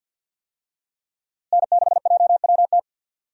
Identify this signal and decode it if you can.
Morse code, heard as a keyed tone, about 52 wpm, 700 Hz. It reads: N6JWT
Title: USA Call Signs - 5 Characters - Repeated 2x - 50wpm